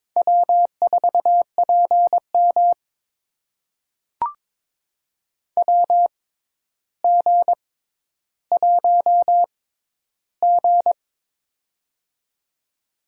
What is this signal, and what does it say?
Morse code, heard as a keyed tone, about 22 wpm, spaced as for 11 wpm, 700 Hz. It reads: W4PM WG1G
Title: USA Call Signs - 4 Characters - 6x Character Spacing - 22wpm